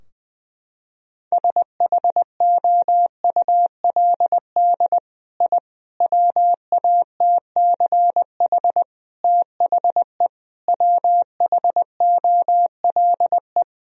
Morse code, heard as a keyed tone, about 20 wpm, 700 Hz. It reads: SHOULD I WATCH THE WHOLE
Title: Sentences from Top 1000 Words - Review Part 4 - 20wpm